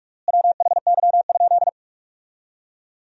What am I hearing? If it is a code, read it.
Morse code, heard as a keyed tone, about 46 wpm, 700 Hz. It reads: WHY?
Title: 3x Word-Spacing - Sentences from Top 200 Words - 45wpm